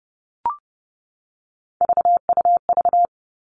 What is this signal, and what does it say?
Morse code, heard as a keyed tone, about 30 wpm, 700 Hz. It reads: VUV
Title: Three Challenging Letter-Number Combinations 30wpm